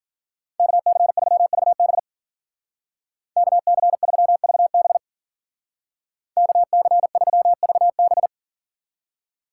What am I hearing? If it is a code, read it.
Morse code, heard as a keyed tone, about 49 wpm, 700 Hz. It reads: KC3VB KC3VB KC3VB
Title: Speed Racing - USA 2x2 Call Signs 40wpm